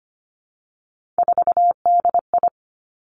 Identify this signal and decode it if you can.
Morse code, heard as a keyed tone, about 25 wpm, 700 Hz. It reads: 4DI
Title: Three Challenging Letter-Number Combinations 25wpm